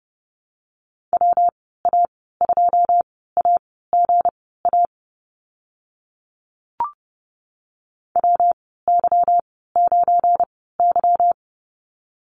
Morse code, heard as a keyed tone, about 30 wpm, spaced as for 17 wpm, 700 Hz. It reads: WA2AGA WY9Y
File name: Lesson 37g - Letter x - Call Signs 30wpm